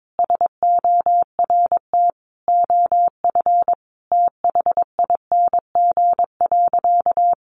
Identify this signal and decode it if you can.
Morse code, heard as a keyed tone, about 22 wpm, 700 Hz. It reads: SORT OF THING.
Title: Sentences from Top 1300 Words 22wpm